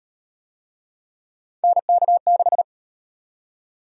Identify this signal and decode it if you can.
Morse code, heard as a keyed tone, about 33 wpm, 700 Hz. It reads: NK6